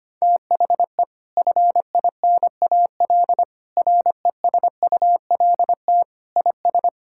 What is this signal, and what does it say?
Morse code, heard as a keyed tone, about 25 wpm, 700 Hz. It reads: THE FINAL RESULT IS
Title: Sentences from Top 1000 Words - Review Part 5 - 25wpm